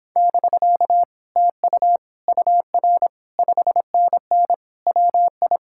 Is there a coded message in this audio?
Yes: Morse code from 0.2 to 5.6 s, heard as a keyed tone, about 31 wpm, 700 Hz. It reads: <BK> TU UR 5NN WI